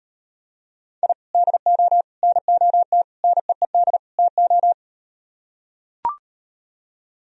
Morse code, heard as a keyed tone, about 38 wpm, 700 Hz. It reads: I DO NOT NEED TO